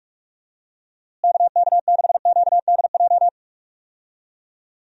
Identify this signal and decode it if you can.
Morse code, heard as a keyed tone, about 43 wpm, 700 Hz. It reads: KK6QDJ
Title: USA Call Signs - 6 Characters - Repeated 3x - 45wpm